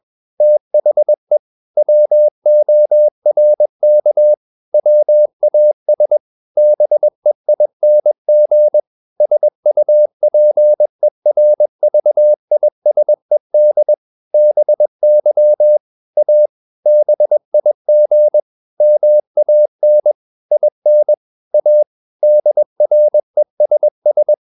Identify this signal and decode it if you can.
Morse code, heard as a keyed tone, about 21 wpm, 600 Hz. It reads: THE WORK WAS BEING SUPERVISED BY A BIG MAN IN A DRESS
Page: CW med Gnister